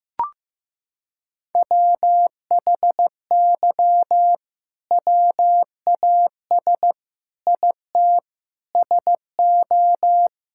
Morse code, heard as a keyed tone, about 15 wpm, 700 Hz. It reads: WHY WAS IT SO
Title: Sentences from Top 1000 Words - Review Part 5 - 15wpm